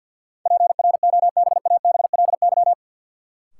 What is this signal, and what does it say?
Morse code, heard as a keyed tone, about 51 wpm, 700 Hz. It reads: PROBABLY
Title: Sentences from Top 700 Words - New Word Drill - 50wpm